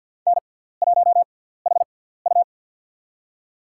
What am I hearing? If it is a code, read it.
Morse code, heard as a keyed tone, about 53 wpm, spaced as for 17 wpm, 700 Hz. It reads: N1HU
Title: USA Call Signs - 6x Character Spacing - 50wpm